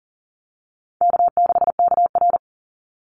Morse code, heard as a keyed tone, about 37 wpm, 700 Hz. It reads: K6KR
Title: Easy Call Signs - 40wpm